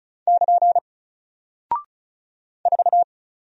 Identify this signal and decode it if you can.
Morse code, heard as a keyed tone, about 36 wpm, 700 Hz. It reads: <KN> 4
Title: Single Character (Letter:Number:Punctuation:Prosign) Alt 35wpm